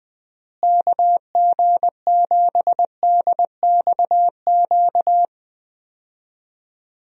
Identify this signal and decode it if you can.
Morse code, heard as a keyed tone, about 20 wpm, 700 Hz. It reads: KG7DXQ
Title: USA Call Signs - 6 Characters - Repeated 2x - 20wpm